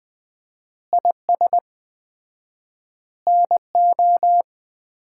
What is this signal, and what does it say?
Morse code, heard as a keyed tone, about 20 wpm, 700 Hz. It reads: IS NO
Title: 4x Word-Spacing - Sets of 5 Words - 20wpm